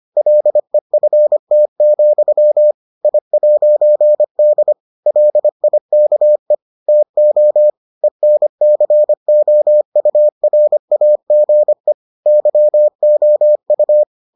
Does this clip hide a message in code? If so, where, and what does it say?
Yes: Morse code from 0.2 to 14.0 s, heard as a keyed tone, about 25 wpm, 600 Hz. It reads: LEFT, I'D LIKE TO ENCOURAGE YOU